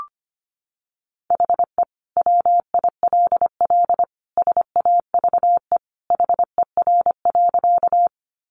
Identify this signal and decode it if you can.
Morse code, heard as a keyed tone, about 25 wpm, 700 Hz. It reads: HE WILL SAVE HER.
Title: Sentences from Top 1100 Words 25wpm